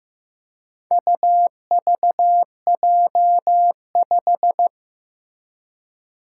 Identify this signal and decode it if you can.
Morse code, heard as a keyed tone, about 15 wpm, 700 Hz. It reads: UVJ5
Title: Four Challenging Letter-Number Combinations 15wpm